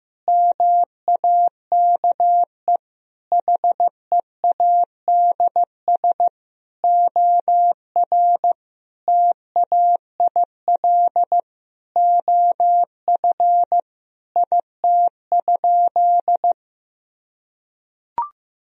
Morse code, heard as a keyed tone, about 15 wpm, 700 Hz. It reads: MAKE HEADS OR TAIL OF IT?
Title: Sentences from Top 1000 Words - Review Part 3 - 15wpm